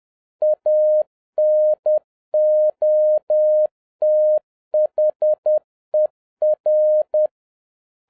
Morse code, heard as a keyed tone, about 10 wpm, 610 Hz. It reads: ANOTHER